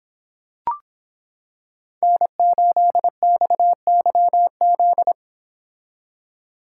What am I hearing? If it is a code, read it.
Morse code, heard as a keyed tone, about 26 wpm, 700 Hz. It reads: N8XYZ